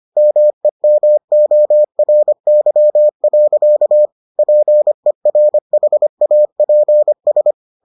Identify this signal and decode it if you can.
Morse code, heard as a keyed tone, about 25 wpm, 600 Hz. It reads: MEMORY. PERHAPS